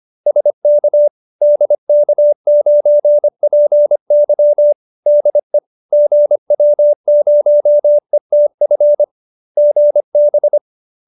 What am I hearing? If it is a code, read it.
Morse code, heard as a keyed tone, about 25 wpm, 590 Hz. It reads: SK DK9PY DE GW0ETF GB